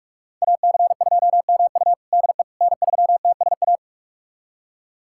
Morse code, heard as a keyed tone, about 44 wpm, 700 Hz. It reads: AC2MU DE N3TSA
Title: USA Callsign DE Callsign - 5 Characters - 45wpm